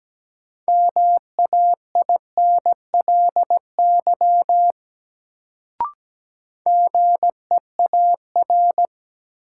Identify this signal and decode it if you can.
Morse code, heard as a keyed tone, about 17 wpm, 700 Hz. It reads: MAINLY GEAR
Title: Sentences from Top 1400 Words - New Word Drill - 17wpm